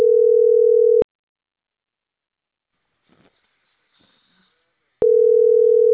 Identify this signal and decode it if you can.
Ringback tone (North America), 440 + 480 Hz.